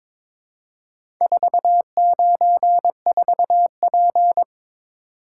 Morse code, heard as a keyed tone, about 22 wpm, 700 Hz. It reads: 494P
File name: practice / 22wpm / Four Letter-Number Combinations - 22wpm